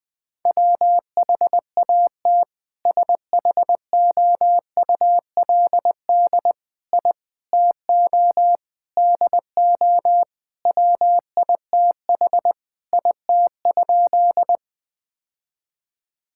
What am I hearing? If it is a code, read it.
Morse code, heard as a keyed tone, about 20 wpm, 700 Hz. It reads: WHAT SHOULD I TO DO WITH IT?
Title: Sentences with 4-Word Phrases - 20wpm